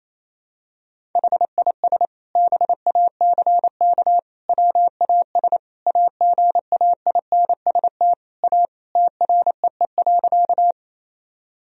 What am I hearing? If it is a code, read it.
Morse code, heard as a keyed tone, about 28 wpm, 700 Hz. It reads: HIS BACK WAS AGAINST A TREE.
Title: Sentences from Top 1000 Words - Review Part 1 - 28wpm